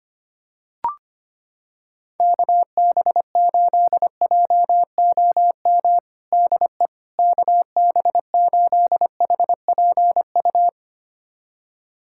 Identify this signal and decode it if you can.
Morse code, heard as a keyed tone, about 25 wpm, 700 Hz. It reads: KB8JOM DE KB8HPU
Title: USA Callsign DE Callsign - 6 Characters - 25wpm